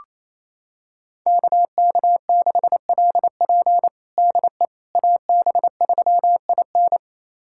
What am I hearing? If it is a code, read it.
Morse code, heard as a keyed tone, about 28 wpm, 700 Hz. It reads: KK6LP DE AB3IN
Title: USA Callsign DE Callsign - 5 Characters - 28wpm